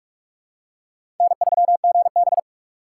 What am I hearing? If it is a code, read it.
Morse code, heard as a keyed tone, about 46 wpm, 700 Hz. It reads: N3GB
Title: USA Call Signs - 4 Characters - 4x Character Spacing - 45wpm